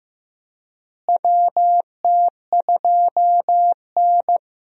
Morse code, heard as a keyed tone, about 15 wpm, 700 Hz. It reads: WT2N
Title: USA Call Signs - 4 Characters - Repeated 2x - 15wpm